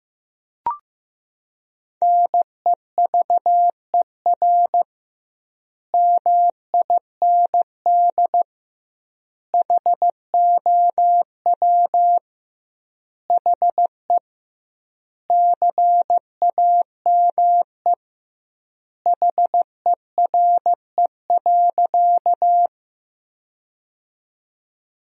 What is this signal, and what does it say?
Morse code, heard as a keyed tone, about 15 wpm, 700 Hz. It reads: NEVER MIND HOW HE CAME HERE.
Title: 2x Word-Spacing - Sentences from Top 500 Words - 15wpm